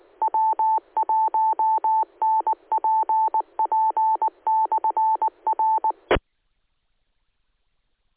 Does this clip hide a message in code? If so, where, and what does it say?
Yes: Morse code from 0.2 to 5.9 s, heard as a keyed tone, about 19 wpm, 880 Hz. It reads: W1NPP/R